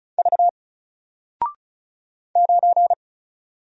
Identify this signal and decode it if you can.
Morse code, heard as a keyed tone, about 36 wpm, 700 Hz. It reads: V 9
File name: Single-Letter-Number-35wpm